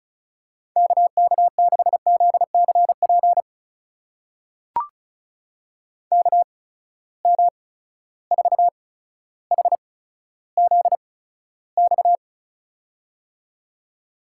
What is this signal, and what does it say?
Morse code, heard as a keyed tone, about 36 wpm, spaced as for 15 wpm, 700 Hz. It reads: KK6ZCP KM4HZX
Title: USA Call Signs - 6 Characters - 8x Character Spacing - 35wpm